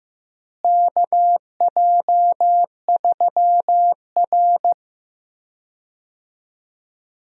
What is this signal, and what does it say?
Morse code, heard as a keyed tone, about 15 wpm, 700 Hz. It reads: KJ3R
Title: USA Call Signs - 4 Characters - Repeated 3x - 15wpm